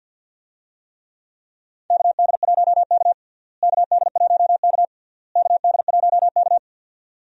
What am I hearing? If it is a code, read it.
Morse code, heard as a keyed tone, about 49 wpm, 700 Hz. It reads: KD1K KD1K KD1K
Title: USA Call Signs - 4 Characters - Repeated 3x - 50wpm